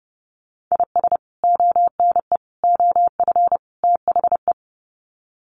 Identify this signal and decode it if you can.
Morse code, heard as a keyed tone, about 30 wpm, 700 Hz. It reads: IS ONE OF THE
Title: Sets of 4 Words 30wpm